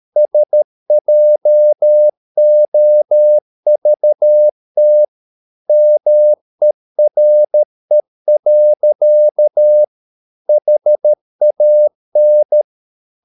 Morse code, heard as a keyed tone, about 13 wpm, 600 Hz. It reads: SJOVT MERE. HAN
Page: Ferie 13 wpm